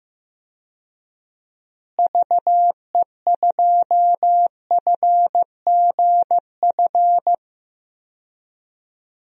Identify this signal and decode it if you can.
Morse code, heard as a keyed tone, about 15 wpm, 700 Hz. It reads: VE2FGF
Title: DXCC-100 Call Signs - 6 Characters - Repeated 2x - 15wpm